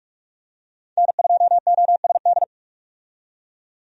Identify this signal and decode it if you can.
Morse code, heard as a keyed tone, about 47 wpm, 700 Hz. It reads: N2OSD
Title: USA Call Signs - 5 Characters - 45wpm